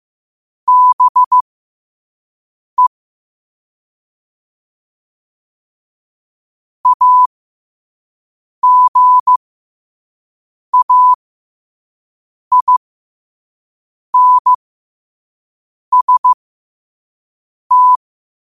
Morse code, heard as a keyed tone, about 15 wpm, spaced as for 5 wpm, 1000 Hz. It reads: BE AGAINST
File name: qotdfiles / 5wpm